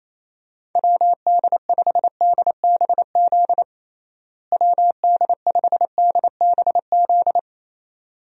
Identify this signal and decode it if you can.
Morse code, heard as a keyed tone, about 28 wpm, 700 Hz. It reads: WD5DBZ WD5DBZ
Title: USA Call Signs - 6 Characters - Repeated 2x - 28wpm